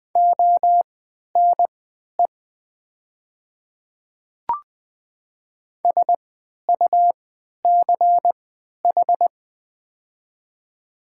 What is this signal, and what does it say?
Morse code, heard as a keyed tone, about 20 wpm, spaced as for 11 wpm, 700 Hz. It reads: ONE SUCH
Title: 3x Character Spacing - Top 200 Words - 20wpm